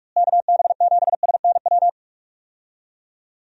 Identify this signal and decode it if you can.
Morse code, heard as a keyed tone, about 43 wpm, 700 Hz. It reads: KB7SNW
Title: USA Call Signs - 6 Characters - Repeated 3x - 45wpm